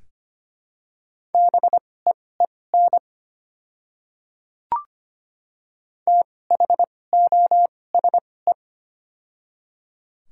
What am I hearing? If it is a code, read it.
Morse code, heard as a keyed tone, about 25 wpm, spaced as for 18 wpm, 700 Hz. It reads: BEEN THOSE
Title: 2x Character Spacing - Top 300 Words - 25wpm